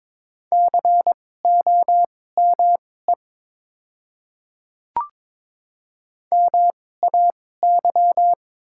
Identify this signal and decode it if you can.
Morse code, heard as a keyed tone, about 22 wpm, spaced as for 16 wpm, 700 Hz. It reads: COME MAY
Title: 2x Character Spacing - Top 100 Words - 22wpm